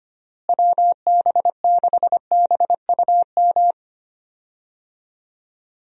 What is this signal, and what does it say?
Morse code, heard as a keyed tone, about 25 wpm, 700 Hz. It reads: WB6BUM
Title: USA Call Signs - 6 Characters - Repeated 2x - 25wpm